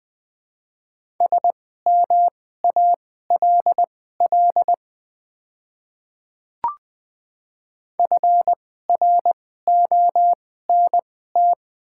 Morse code, heard as a keyed tone, about 20 wpm, spaced as for 15 wpm, 700 Hz. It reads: SMALL FRONT
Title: 2x Character Spacing - Top 500 Words - 20wpm